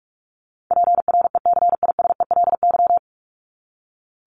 Morse code, heard as a keyed tone, about 44 wpm, 700 Hz. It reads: PRECISELY
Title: LY Suffix - Top 100 Words with Base Words 45wpm